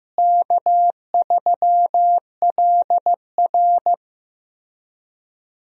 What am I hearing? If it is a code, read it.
Morse code, heard as a keyed tone, about 15 wpm, 700 Hz. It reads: K3LR
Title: USA Call Signs - 4 Characters - Repeated 3x - 15wpm